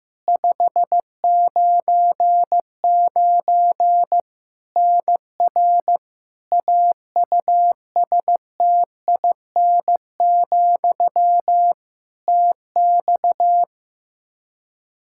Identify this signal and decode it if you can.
Morse code, heard as a keyed tone, about 15 wpm, 700 Hz. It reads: 599 NR AUSTIN, TX
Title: Practice QSO - Exchange 1 - Beginning - 15wpm